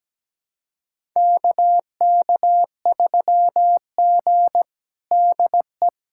Morse code, heard as a keyed tone, about 17 wpm, 700 Hz. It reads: KK3G DE
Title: USA Callsign DE Callsign - 4 Characters - 17wpm